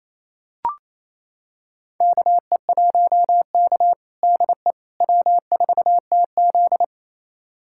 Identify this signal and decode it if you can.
Morse code, heard as a keyed tone, about 28 wpm, 700 Hz. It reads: KE1K DE W4TZ
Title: USA Callsign DE Callsign - 4 Characters - 28wpm